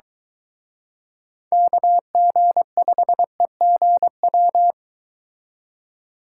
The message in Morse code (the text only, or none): KG5EGW